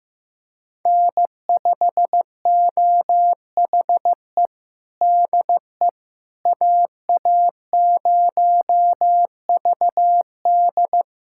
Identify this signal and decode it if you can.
Morse code, heard as a keyed tone, about 16 wpm, 700 Hz. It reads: N5OHE DE AA0VD
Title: USA Callsign DE Callsign - 5 Characters - 15wpm